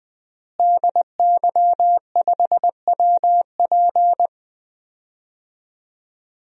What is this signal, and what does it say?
Morse code, heard as a keyed tone, about 20 wpm, 700 Hz. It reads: DY5WP
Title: DXCC-100 Call Signs - 5 Characters - Repeated 3x - 20wpm